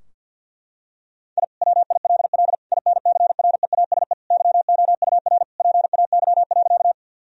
Morse code, heard as a keyed tone, about 52 wpm, 700 Hz. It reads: I WILL INCREASE YOUR PAY.
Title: Sentences from Top 1000 Words - Review Part 4 - 50wpm